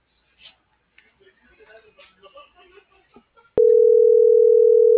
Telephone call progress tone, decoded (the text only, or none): ringback tone